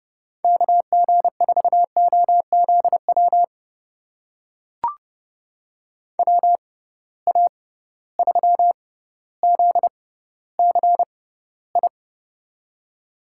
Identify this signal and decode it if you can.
Morse code, heard as a keyed tone, about 30 wpm, spaced as for 15 wpm, 700 Hz. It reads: KG4OZW WA3ZCI
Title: USA Call Signs - 6x Character Spacing - 30wpm